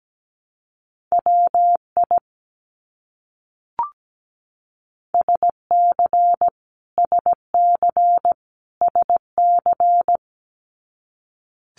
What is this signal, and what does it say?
Morse code, heard as a keyed tone, about 17 wpm, 700 Hz. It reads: WI SC SC SC
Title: US States Repeated 3x 17wpm